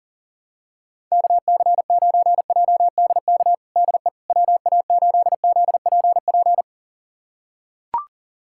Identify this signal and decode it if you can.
Morse code, heard as a keyed tone, about 40 wpm, 700 Hz. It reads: KC9JDK DE WA8ZPP
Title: USA Callsign DE Callsign - 6 Characters - 40wpm